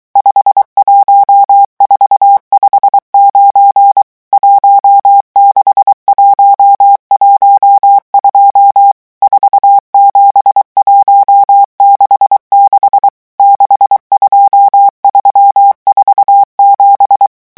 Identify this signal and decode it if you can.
Morse code, heard as a keyed tone, about 23 wpm, 800 Hz. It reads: 51459 16112 47166 62347
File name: morse mp3 / revision11 chiffres vitesse 22